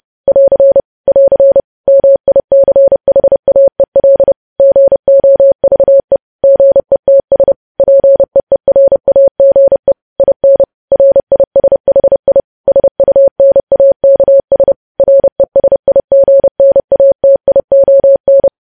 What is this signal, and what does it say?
Morse code, heard as a keyed tone, about 30 wpm, 550 Hz. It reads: <AR> <AR> MICHAEL GOVE GETS PEERAGE IN RISHI SUNAKS RESIGNATION
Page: news headlines in morse code at 30 wpm